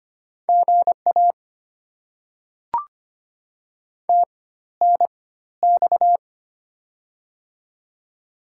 Morse code, heard as a keyed tone, about 25 wpm, spaced as for 14 wpm, 700 Hz. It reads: GA TNX
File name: Top 90 QSO Elements - 4x Character Spacing - 25wpm